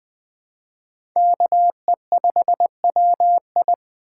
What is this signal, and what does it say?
Morse code, heard as a keyed tone, about 20 wpm, 700 Hz. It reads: KE5WI
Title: USA Call Signs - 5 Characters - 3x Character Spacing - 20wpm